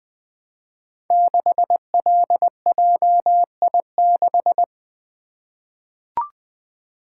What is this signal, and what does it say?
Morse code, heard as a keyed tone, about 20 wpm, 700 Hz. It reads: 6LJI6
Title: Code Groups - 5 Letter:Numbers - 20wpm